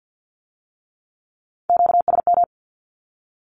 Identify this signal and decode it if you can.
Morse code, heard as a keyed tone, about 50 wpm, 700 Hz. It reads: QSM
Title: Q-Codes with Spoken Meaning - 50wpm